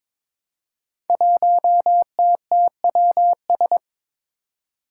Morse code, heard as a keyed tone, about 22 wpm, 700 Hz. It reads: 1TTWS
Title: Five Challenging Letter-Number Combinations 22wpm